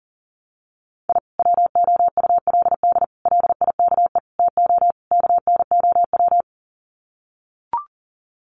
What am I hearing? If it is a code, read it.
Morse code, heard as a keyed tone, about 40 wpm, 700 Hz. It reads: I WOULD LIKE TO KNOW
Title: Sets of 5 Words Encore 40wpm